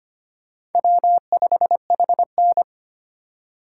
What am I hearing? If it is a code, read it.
Morse code, heard as a keyed tone, about 33 wpm, 700 Hz. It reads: W5HN